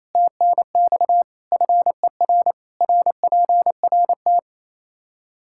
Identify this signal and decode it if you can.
Morse code, heard as a keyed tone, about 28 wpm, 700 Hz. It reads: TNX FER RPRT
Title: Practice QSO - Exchange 2 - Info - 28wpm